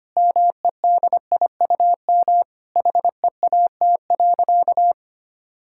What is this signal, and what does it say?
Morse code, heard as a keyed tone, about 25 wpm, 700 Hz. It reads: MEDIUM HEAT.
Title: Sentences from Top 1000 Words - Review Part 6 - 25wpm